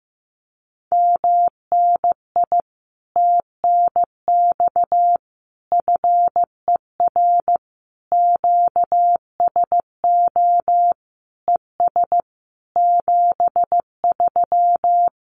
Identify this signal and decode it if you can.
Morse code, heard as a keyed tone, about 14 wpm, 700 Hz. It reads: MNI TNX FER QSO ES 73
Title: Common QSO Phrases Encore 15wpm